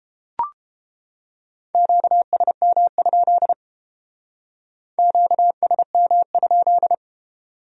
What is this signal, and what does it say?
Morse code, heard as a keyed tone, about 28 wpm, 700 Hz. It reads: QSM? QSM?